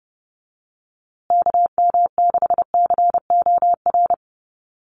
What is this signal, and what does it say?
Morse code, heard as a keyed tone, about 30 wpm, 700 Hz. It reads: KM6COR
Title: USA Call Signs Repeated 3x 30wpm